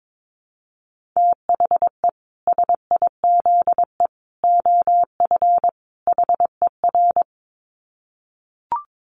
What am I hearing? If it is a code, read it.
Morse code, heard as a keyed tone, about 22 wpm, 700 Hz. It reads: THE SIZE OF HER
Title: Dit Challenge 22wpm